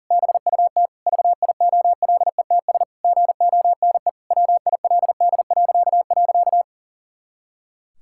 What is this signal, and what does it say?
Morse code, heard as a keyed tone, about 37 wpm, 700 Hz. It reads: BUT VIOLETS GONE WILD..
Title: Pangram Challenge v2 - 40wpm